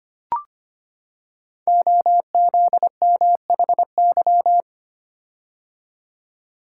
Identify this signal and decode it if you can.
Morse code, heard as a keyed tone, about 25 wpm, 700 Hz. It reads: OZMHY
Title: Code Groups - 5 Letters - 25wpm